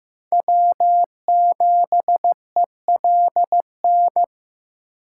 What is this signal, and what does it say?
Morse code, heard as a keyed tone, about 15 wpm, 700 Hz. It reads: W7ELN
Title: USA Call Signs - 5 Characters - 2x Character Spacing - 15wpm